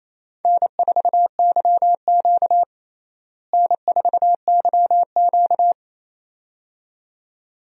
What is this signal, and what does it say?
Morse code, heard as a keyed tone, about 28 wpm, 700 Hz. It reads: N4YQ N4YQ
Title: USA Call Signs - 4 Characters - Repeated 2x - 28wpm